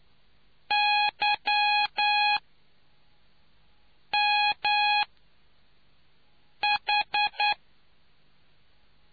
Morse code, heard as a keyed tone, about 9 wpm, 830 Hz. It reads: Y M H